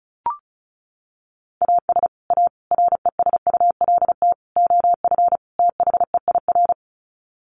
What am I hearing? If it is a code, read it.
Morse code, heard as a keyed tone, about 34 wpm, 700 Hz. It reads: AS A RESULT OF THEIR
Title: Sets of 5 Words Encore2 35wpm